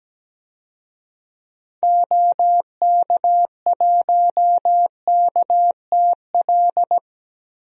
Morse code, heard as a keyed tone, about 17 wpm, 700 Hz. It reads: OK1KTL
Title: DXCC-100 Call Signs - 6 Characters - Repeated 2x - 17wpm